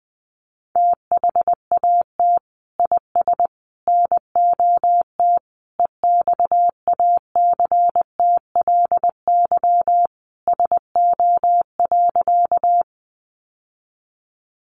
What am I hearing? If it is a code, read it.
Morse code, heard as a keyed tone, about 20 wpm, 700 Hz. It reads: THAT IS NOT EXACTLY SO.
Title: Sentences from Top 900 Words 20wpm